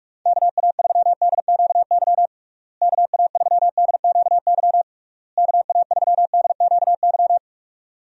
Morse code, heard as a keyed tone, about 41 wpm, 700 Hz. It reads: KA3DQY KA3DQY KA3DQY
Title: USA Call Signs - 6 Characters - Repeated 3x - 45wpm